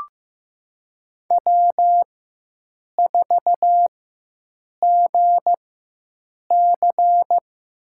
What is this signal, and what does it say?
Morse code, heard as a keyed tone, about 15 wpm, spaced as for 7 wpm, 700 Hz. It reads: W4GC
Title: USA Call Signs - 4x Character Spacing - 15wpm